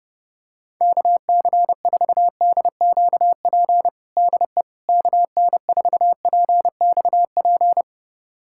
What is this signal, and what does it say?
Morse code, heard as a keyed tone, about 32 wpm, 700 Hz. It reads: KC4DQP DE KN4PXP
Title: USA Callsign DE Callsign 30wpm